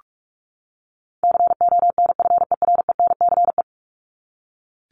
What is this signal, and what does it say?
Morse code, heard as a keyed tone, about 44 wpm, 700 Hz. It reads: CONFERENCE
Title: Top 1751-2000 Words 45wpm